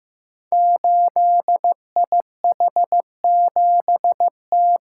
Morse code, heard as a keyed tone, about 15 wpm, 700 Hz. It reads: 8IH7T
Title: Code Groups - 5 Letter:Numbers - 15wpm